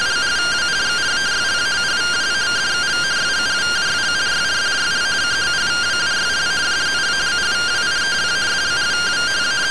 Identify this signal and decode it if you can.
Two-tone teleprinter signal, about 170 Hz shift.